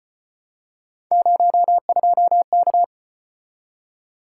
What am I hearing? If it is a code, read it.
Morse code, heard as a keyed tone, about 33 wpm, 700 Hz. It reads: 02K